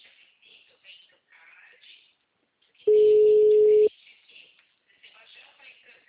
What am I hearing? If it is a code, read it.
Ringback tone (Europe, 425 Hz), with speech.